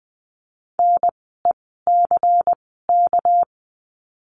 Morse code, heard as a keyed tone, about 20 wpm, spaced as for 14 wpm, 700 Hz. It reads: NECK
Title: 199 Lesson 32f - Letter k - Words 20wpm